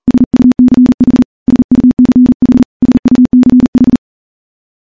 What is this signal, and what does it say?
Morse code, heard as a keyed tone, about 40 wpm, 250 Hz. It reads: SUCH SUCH SUCH
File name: morse-training-40WPM